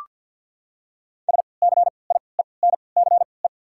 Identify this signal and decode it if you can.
Morse code, heard as a keyed tone, about 53 wpm, spaced as for 29 wpm, 700 Hz. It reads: SCIENCE
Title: 3x Character Spacing - Top 500 Words - 50wpm